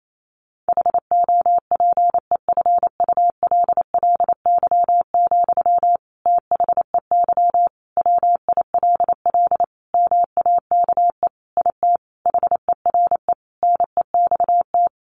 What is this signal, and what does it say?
Morse code, heard as a keyed tone, about 28 wpm, 700 Hz. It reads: HOPEFULLY, THEY WILL MAKE IT HERE NEXT
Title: Sentences from Top 1500 Words 28wpm